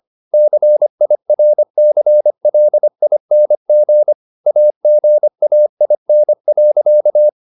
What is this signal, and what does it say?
Morse code, heard as a keyed tone, about 25 wpm, 600 Hz. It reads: CIRCLING AGAIN.
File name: the old man and the sea part 1 25wpm